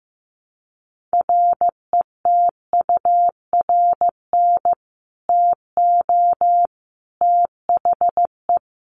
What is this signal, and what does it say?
Morse code, heard as a keyed tone, about 15 wpm, 700 Hz. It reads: RETURN TO THE
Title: Sentences from Top 1400 Words 15wpm